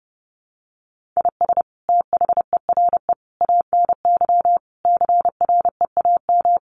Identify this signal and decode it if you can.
Morse code, heard as a keyed tone, about 30 wpm, 700 Hz. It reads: IS THERE ANY CREAM
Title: Sentences from Top 1500 Words 30wpm